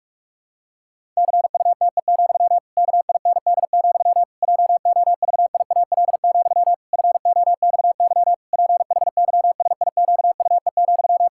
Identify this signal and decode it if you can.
Morse code, heard as a keyed tone, about 41 wpm, 700 Hz. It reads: CUTE, KIND, JOVIAL, FOXY PHYSIQUE,
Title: Pangram Challenge v2 - 45wpm